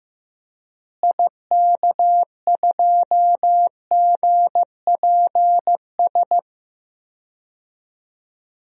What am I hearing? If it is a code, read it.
Morse code, heard as a keyed tone, about 15 wpm, 700 Hz. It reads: IK2GPS
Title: DXCC-100 Call Signs - 6 Characters - Repeated 2x - 15wpm